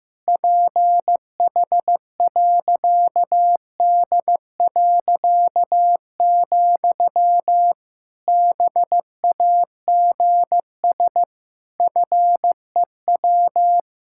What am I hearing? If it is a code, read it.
Morse code, heard as a keyed tone, about 15 wpm, 700 Hz. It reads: PH.D., BAGS FEW
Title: Pangram Challenge v2 - 15wpm